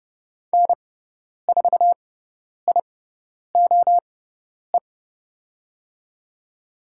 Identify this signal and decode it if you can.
Morse code, heard as a keyed tone, about 30 wpm, spaced as for 10 wpm, 700 Hz. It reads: N4IOE
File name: Lesson 16g - Letter c - Call Signs 30wpm